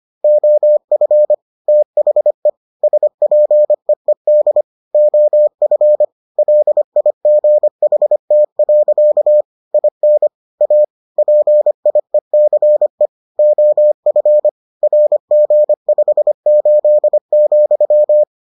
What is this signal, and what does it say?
Morse code, heard as a keyed tone, about 25 wpm, 600 Hz. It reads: OF THE SPEED OF LIGHT. IN A PIECE OF RG58,